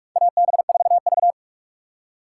Morse code, heard as a keyed tone, about 46 wpm, 700 Hz. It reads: AB4V
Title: USA Call Signs - 4 Characters - 4x Character Spacing - 45wpm